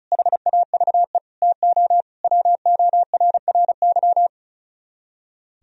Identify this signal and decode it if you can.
Morse code, heard as a keyed tone, about 44 wpm, 700 Hz. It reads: HAVE TO WORRY